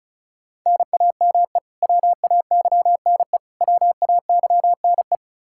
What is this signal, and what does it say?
Morse code, heard as a keyed tone, about 36 wpm, 700 Hz. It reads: NAME WAYNE WAYNE
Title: Practice QSO - Exchange 1 - Beginning - 35wpm